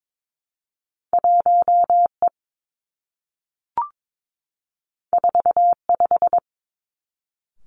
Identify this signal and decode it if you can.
Morse code, heard as a keyed tone, about 22 wpm, 700 Hz. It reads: 1 E 4 5
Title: Two Challenging Letter-Number Combinations 22wpm